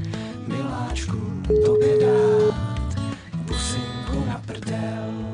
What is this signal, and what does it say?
Ringback tone (Europe, 425 Hz), with noise.